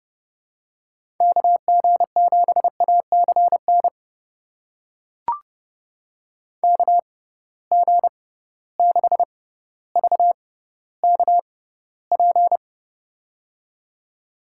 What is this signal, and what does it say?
Morse code, heard as a keyed tone, about 30 wpm, spaced as for 15 wpm, 700 Hz. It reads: KG7ACN KG6VKP
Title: USA Call Signs - 6 Characters - 6x Character Spacing - 30wpm